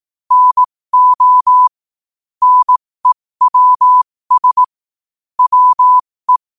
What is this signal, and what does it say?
Morse code, heard as a keyed tone, about 18 wpm, spaced as for 15 wpm, 1000 Hz. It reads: NO NEWS WE
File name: qotdfiles / 15wpm